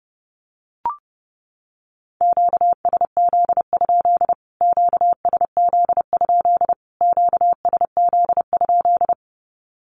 Morse code, heard as a keyed tone, about 30 wpm, 700 Hz. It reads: QSZ? QSZ? QSZ?
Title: Q-Codes Repeated 3x 30wpm